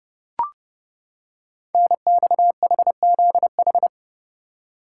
Morse code, heard as a keyed tone, about 28 wpm, 700 Hz. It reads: NXHZH
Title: Code Groups - 5 Letters - 30wpm